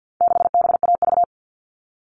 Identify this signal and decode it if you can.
Morse code, heard as a keyed tone, about 48 wpm, 700 Hz. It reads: 6BAV
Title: All Cut Numbers - 4-Digit 50wpm